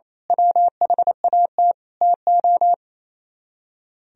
Morse code, heard as a keyed tone, about 29 wpm, 700 Hz. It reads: WHAT TO
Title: Speed-Racing - Sets of 5 Words 28wpm